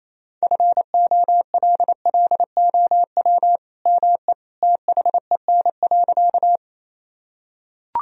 Morse code, heard as a keyed tone, about 28 wpm, 700 Hz. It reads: FOLLOW ME THEN.
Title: Sentences from Top 1000 Words - Review Part 1 - 28wpm